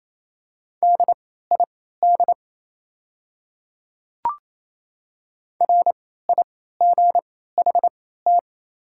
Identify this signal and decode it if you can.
Morse code, heard as a keyed tone, about 28 wpm, spaced as for 16 wpm, 700 Hz. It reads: DID RIGHT
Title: 3x Character Spacing - Top 200 Words - 28wpm